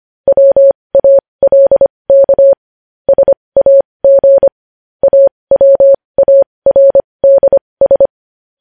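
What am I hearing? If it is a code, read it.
Morse code, heard as a keyed tone, about 25 wpm, spaced as for 20 wpm, 550 Hz. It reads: WALK SAG AWARDS
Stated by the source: Morse code news headlines practice at 20 wpm.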